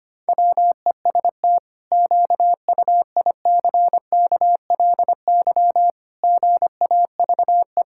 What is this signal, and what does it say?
Morse code, heard as a keyed tone, about 25 wpm, 700 Hz. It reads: WEST QUICKLY GAVE
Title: Pangram Challenge v2 - 25wpm